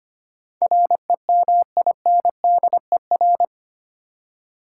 Morse code, heard as a keyed tone, about 25 wpm, 700 Hz. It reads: REMINDER
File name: Eight Letter Words - 25wpm